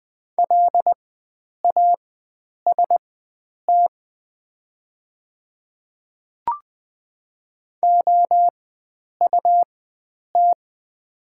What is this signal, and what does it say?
Morse code, heard as a keyed tone, about 20 wpm, spaced as for 9 wpm, 690 Hz. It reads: LAST OUT
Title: 4x Character Spacing - Top 500 Words - 20wpm